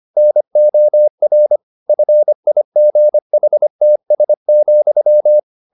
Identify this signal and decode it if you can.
Morse code, heard as a keyed tone, about 25 wpm, 600 Hz. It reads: NOR FIGHTS,
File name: the old man and the sea part 1 25wpm